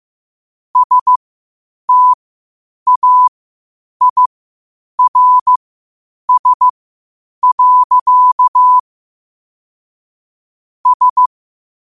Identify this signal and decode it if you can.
Morse code, heard as a keyed tone, about 15 wpm, spaced as for 8 wpm, 1000 Hz. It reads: STAIRS. S
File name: qotdfiles / 8wpm